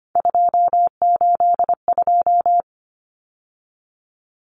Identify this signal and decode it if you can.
Morse code, heard as a keyed tone, about 25 wpm, 700 Hz. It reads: 282
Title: Numbers - 3-Digit 25wpm